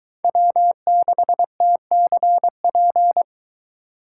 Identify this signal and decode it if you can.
Morse code, heard as a keyed tone, about 22 wpm, 700 Hz. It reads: W6TCP